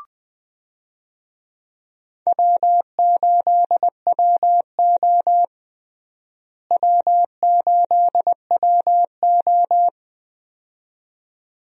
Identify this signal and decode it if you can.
Morse code, heard as a keyed tone, about 21 wpm, 700 Hz. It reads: W8WO W8WO
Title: USA Call Signs - 4 Characters - Repeated 2x - 20wpm